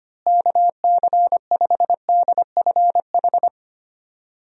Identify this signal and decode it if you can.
Morse code, heard as a keyed tone, about 25 wpm, 700 Hz. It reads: KC5DFH
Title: USA Call Signs - 6 Characters - Repeated 3x - 25wpm